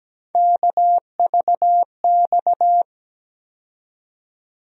Morse code, heard as a keyed tone, about 17 wpm, 700 Hz. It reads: KVX
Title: USA Call Sign Postfixes - 3 Letters 17wpm